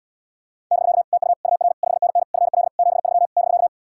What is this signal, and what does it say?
Morse code, heard as a keyed tone, about 79 wpm, 700 Hz. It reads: <BK> TU UR 5NN FL 73 <BK>